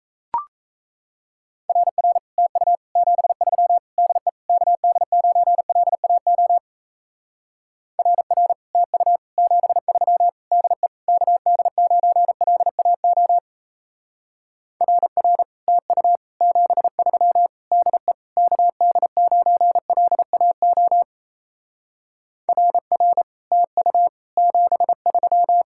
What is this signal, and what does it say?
Morse code, heard as a keyed tone, about 34 wpm, 700 Hz. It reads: RR TU 73 DE KD9LAO RR TU 73 DE KD9LAO RR TU 73 DE KD9LAO RR TU 73
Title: Speed-Racing Common QSO Phrases Encore - 28wpm